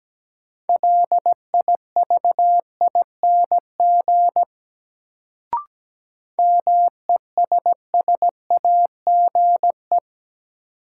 Morse code, heard as a keyed tone, about 17 wpm, 700 Hz. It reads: LIVING MESSAGE
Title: Sentences from Top 1100 Words - New Word Drill Alt - 17wpm